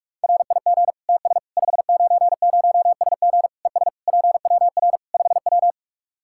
Morse code, heard as a keyed tone, about 41 wpm, 700 Hz. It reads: RIG TS 590SG ES PWR 5W
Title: Practice QSO - Exchange 2 - Info - 45wpm